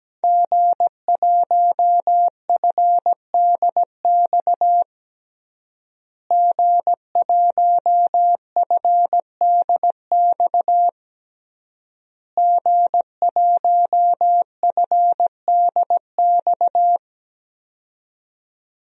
Morse code, heard as a keyed tone, about 17 wpm, 700 Hz. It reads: G1FDX G1FDX G1FDX
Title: DXCC-100 Call Signs - 5 Characters - Repeated 3x - 17wpm